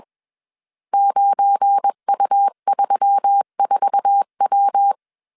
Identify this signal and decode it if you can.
Morse code, heard as a keyed tone, about 21 wpm, 800 Hz. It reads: 9U34W
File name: CW / DET / 1 9 1500C S B 20wpm 2313